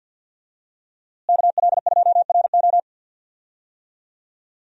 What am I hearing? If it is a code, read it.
Morse code, heard as a keyed tone, about 51 wpm, 700 Hz. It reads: KL2RO
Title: USA Call Signs - 5 Characters - Repeated 3x - 50wpm